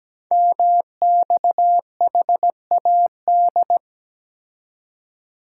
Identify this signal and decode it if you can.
Morse code, heard as a keyed tone, about 17 wpm, 700 Hz. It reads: MXHAD
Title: Code Groups - 5 Letters - 17wpm